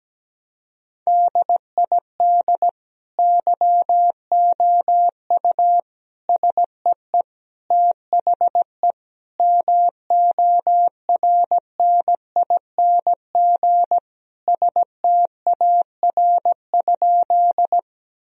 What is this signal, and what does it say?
Morse code, heard as a keyed tone, about 17 wpm, 700 Hz. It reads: DID YOU SEE THE MORNING STAR?
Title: Sentences from Top 1000 Words - Review Part 2 - 17wpm